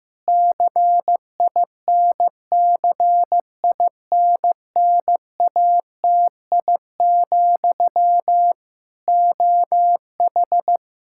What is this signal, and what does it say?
Morse code, heard as a keyed tone, about 15 wpm, 700 Hz. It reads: CINCINNATI, OH
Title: Practice QSO - Exchange 1 - Beginning - 15wpm